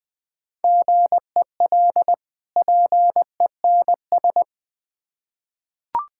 Morse code, heard as a keyed tone, about 20 wpm, 690 Hz. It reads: GEL PENS
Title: Difficult Words with Hints - 20wpm